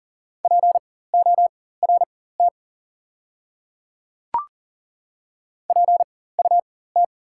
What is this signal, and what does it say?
Morse code, heard as a keyed tone, about 40 wpm, spaced as for 19 wpm, 690 Hz. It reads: PORT PUT
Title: 4x Character Spacing - Top 200 Words - 40wpm